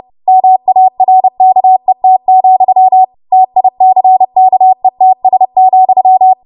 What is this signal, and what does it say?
Morse code, heard as a keyed tone, about 30 wpm, 750 Hz. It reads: MARKET, TICKETS,
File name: Morse / 30WPM